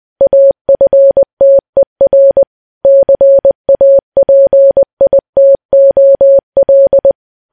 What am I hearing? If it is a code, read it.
Morse code, heard as a keyed tone, about 20 wpm, 550 Hz. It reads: AFTER CAPITOL
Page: news-headlines-in-morse-code-at-20-wpm